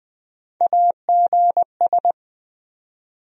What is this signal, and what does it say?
Morse code, heard as a keyed tone, about 20 wpm, 700 Hz. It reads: AGS
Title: USA Call Sign Postfixes - 3 Letters 20wpm